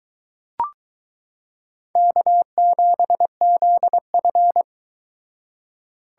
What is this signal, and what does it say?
Morse code, heard as a keyed tone, about 22 wpm, 700 Hz. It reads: K7ZF